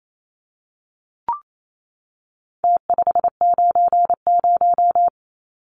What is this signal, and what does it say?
Morse code, heard as a keyed tone, about 28 wpm, 700 Hz. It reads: T590
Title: Numbers - Contest Style - 4-Digit 28wpm